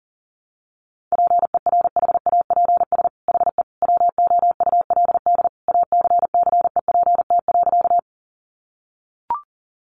Morse code, heard as a keyed tone, about 40 wpm, 700 Hz. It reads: PERHAPS HE WOULD ACCEPT.
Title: Sentences from Top 1300 Words 40wpm